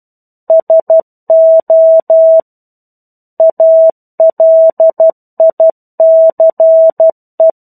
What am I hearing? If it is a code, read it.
Morse code, heard as a keyed tone, about 12 wpm, 650 Hz. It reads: SO ALICE
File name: 01-Chapter-01 12 WPM